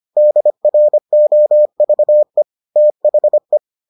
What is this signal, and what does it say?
Morse code, heard as a keyed tone, about 25 wpm, 600 Hz. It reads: DROVE THE